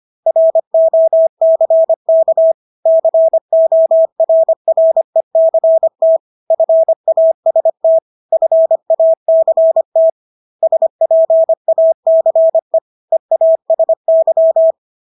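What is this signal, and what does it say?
Morse code, heard as a keyed tone, about 25 wpm, 650 Hz. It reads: ROCK CORRECT FAST FACT SPACE EASY
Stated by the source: Morse practice of English words at 25 wpm from CW med Gnister.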